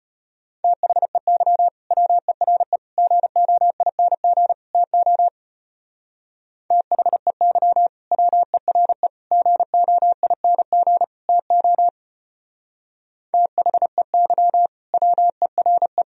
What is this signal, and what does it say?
Morse code, heard as a keyed tone, about 29 wpm, 700 Hz. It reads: THEY WERE GOING TO THEY WERE GOING TO THEY WERE
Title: Speed-Racing - Sets of 4 Words 25wpm